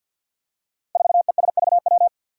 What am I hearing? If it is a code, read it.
Morse code, heard as a keyed tone, about 49 wpm, 700 Hz. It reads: 4ESVW
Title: Five Challenging Letter-Number Combinations 50wpm